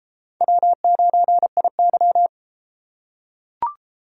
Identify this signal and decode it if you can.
Morse code, heard as a keyed tone, about 31 wpm, 700 Hz. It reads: W9IY